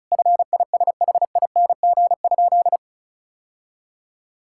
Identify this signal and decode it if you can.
Morse code, heard as a keyed tone, about 36 wpm, 700 Hz. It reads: FISHING?
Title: Sentences from Top 1000 Words - Review Part 5 - 35wpm